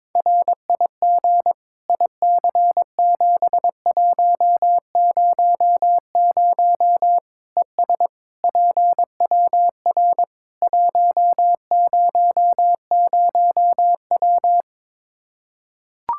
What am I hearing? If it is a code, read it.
Morse code, heard as a keyed tone, about 21 wpm, 700 Hz. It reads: RIG IC7100 ES PWR 100W
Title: Practice QSO - Exchange 2 - Info - 22wpm